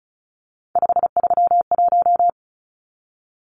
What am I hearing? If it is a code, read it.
Morse code, heard as a keyed tone, about 34 wpm, 700 Hz. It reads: 531
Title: Numbers - 3-Digit 35wpm